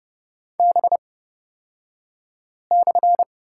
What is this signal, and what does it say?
Morse code, heard as a keyed tone, about 30 wpm, spaced as for 5 wpm, 700 Hz. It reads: B/
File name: Lesson 36d - BT - Two Characters 30wpm